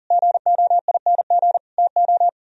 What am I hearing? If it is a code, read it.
Morse code, heard as a keyed tone, about 50 wpm, 700 Hz. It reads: GOING TO